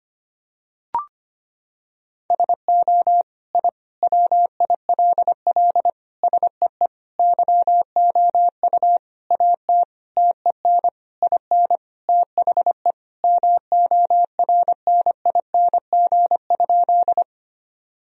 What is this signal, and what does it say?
Morse code, heard as a keyed tone, about 25 wpm, 700 Hz. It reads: SO I WILL SEE YOU AT TEN IN THE MORNING?
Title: Sentences from Top 1000 Words - Review Part 2 - 25wpm